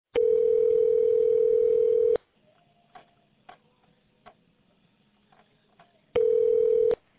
Ringback tone (North America), 440 + 480 Hz.